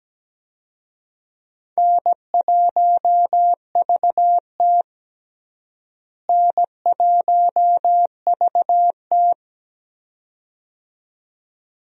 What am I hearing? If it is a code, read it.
Morse code, heard as a keyed tone, about 17 wpm, 700 Hz. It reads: N1VT N1VT
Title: USA Call Signs - 4 Characters - Repeated 2x - 17wpm